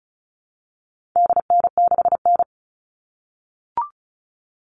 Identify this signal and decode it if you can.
Morse code, heard as a keyed tone, about 34 wpm, 700 Hz. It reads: DN6N
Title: All Cut Numbers - 4-Digit 35wpm